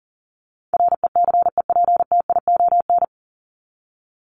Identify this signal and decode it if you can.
Morse code, heard as a keyed tone, about 40 wpm, 700 Hz. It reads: RECEPTION
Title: Top 1751-2000 Words 40wpm